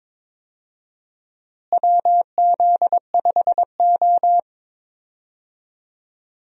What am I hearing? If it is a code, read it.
Morse code, heard as a keyed tone, about 22 wpm, 700 Hz. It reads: WZ5O
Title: USA Call Signs - 4 Characters - Repeated 3x - 22wpm